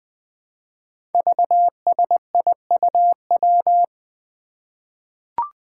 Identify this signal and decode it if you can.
Morse code, heard as a keyed tone, about 21 wpm, 700 Hz. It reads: VSIUW
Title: Five Challenging Letter-Number Combinations 20wpm